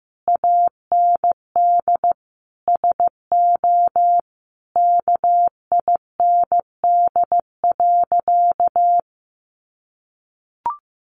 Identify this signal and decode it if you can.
Morse code, heard as a keyed tone, about 15 wpm, 700 Hz. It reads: AND SO KIND.
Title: Sentences from Top 1500 Words 15wpm